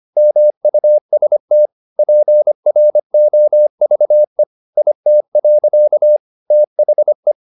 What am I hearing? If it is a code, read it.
Morse code, heard as a keyed tone, about 25 wpm, 600 Hz. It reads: MUST PROVE IT. THE
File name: the old man and the sea part 1 25wpm